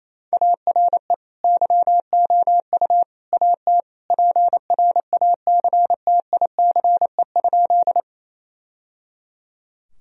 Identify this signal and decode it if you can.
Morse code, heard as a keyed tone, about 28 wpm, 700 Hz. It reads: ARE YOU AT PRACTICE?
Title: Sentences from Top 1000 Words - Review Part 4 - 28wpm